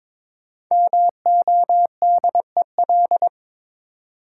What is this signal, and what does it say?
Morse code, heard as a keyed tone, about 22 wpm, 700 Hz. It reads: MODEL
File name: 5 Letter Words - First 3 Letters Given - 22wpm